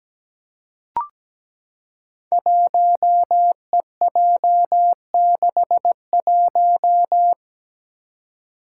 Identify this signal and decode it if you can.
Morse code, heard as a keyed tone, about 17 wpm, 700 Hz. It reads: 1EJ61
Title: Five Challenging Letter-Number Combinations 17wpm